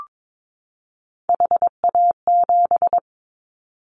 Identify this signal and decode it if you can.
Morse code, heard as a keyed tone, about 22 wpm, 700 Hz. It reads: HA7
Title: Callsign Prefixes for Top 25 DXCC - 22wpm